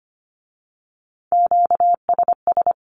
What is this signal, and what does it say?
Morse code, heard as a keyed tone, about 25 wpm, 700 Hz. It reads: QSS
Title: Q-Codes with Spoken Meaning - 25wpm